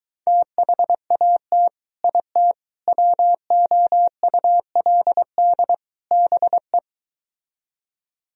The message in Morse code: THAT IT WOULD BE

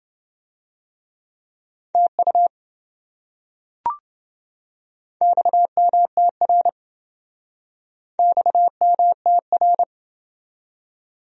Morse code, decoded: TU XMTR XMTR

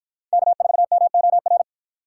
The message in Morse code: K4MOR

